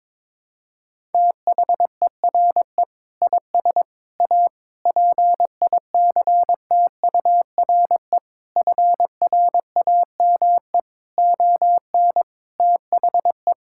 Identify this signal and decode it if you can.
Morse code, heard as a keyed tone, about 22 wpm, 700 Hz. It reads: THERE IS A PICTURE FRAME ON THE
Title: Sentences from Top 1000 Words - Review Part 6 - 22wpm